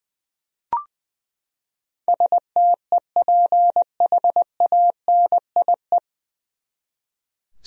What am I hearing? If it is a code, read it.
Morse code, heard as a keyed tone, about 20 wpm, 700 Hz. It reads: STEPHANIE
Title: USA Common Names - 20wpm